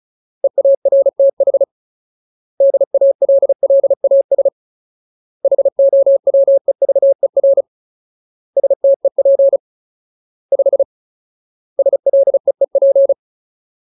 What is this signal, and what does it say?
Morse code, heard as a keyed tone, about 34 wpm, 550 Hz. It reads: EARTH DALLAS HOWEVER STEP 5 SLEEP
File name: Ultimate Word List / 35WPM Ultimate